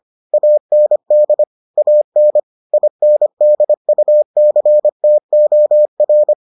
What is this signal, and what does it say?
Morse code, heard as a keyed tone, about 25 wpm, 600 Hz. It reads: AND AN INDUCTOR